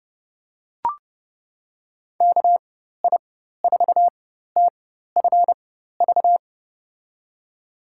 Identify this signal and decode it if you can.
Morse code, heard as a keyed tone, about 30 wpm, spaced as for 14 wpm, 700 Hz. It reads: KI4TFV
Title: USA Call Signs - 4x Character Spacing - 30wpm